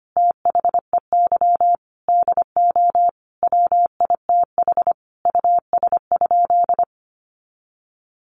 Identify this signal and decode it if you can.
Morse code, heard as a keyed tone, about 25 wpm, 700 Hz. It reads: THEY DO WITH US?
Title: Sentences from Top 200 Words 25wpm